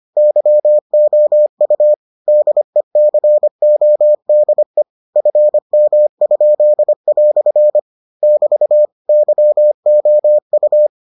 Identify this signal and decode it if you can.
Morse code, heard as a keyed tone, about 25 wpm, 600 Hz. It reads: YOU DECODE FM?" <BT> YOU